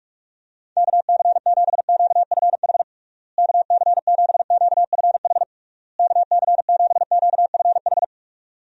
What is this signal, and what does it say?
Morse code, heard as a keyed tone, about 42 wpm, 700 Hz. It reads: KC7QFH KC7QFH KC7QFH
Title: USA Call Signs - 6 Characters - Repeated 3x - 45wpm